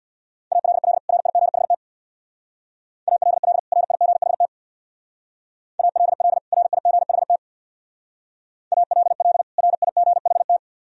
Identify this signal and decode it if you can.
Morse code, heard as a keyed tone, about 65 wpm, 700 Hz. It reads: ALL RIGHT ALL RIGHT ALL RIGHT ALL RIGHT